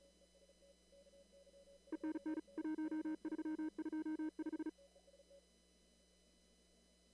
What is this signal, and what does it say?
Morse code, heard as a keyed tone, about 36 wpm, 310 Hz. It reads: ENN 1325
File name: CW / kt5j wpx cw-op-k5pi